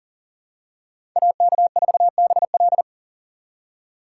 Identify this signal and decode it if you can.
Morse code, heard as a keyed tone, about 40 wpm, 700 Hz. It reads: AK4BL
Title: USA Call Signs - 2x Character Spacing - 40wpm